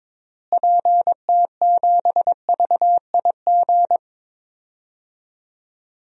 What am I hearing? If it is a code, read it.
Morse code, heard as a keyed tone, about 22 wpm, 700 Hz. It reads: PT7VIG
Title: DXCC-100 Call Signs - 6 Characters - Repeated 3x - 22wpm